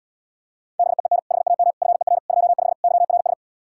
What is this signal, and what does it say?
Morse code, heard as a keyed tone, about 77 wpm, 700 Hz. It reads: BEEN HAM FER 25 YRS